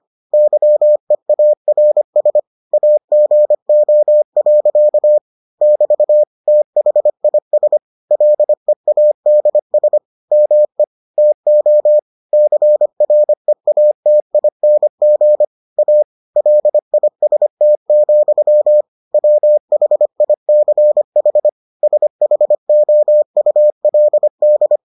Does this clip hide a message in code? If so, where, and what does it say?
Yes: Morse code from 0.3 to 24.8 s, heard as a keyed tone, about 25 wpm, 600 Hz. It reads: YEARS AGO. <BT> THIS LEADS ME TO CREATING A LIST, WHICH SHOULD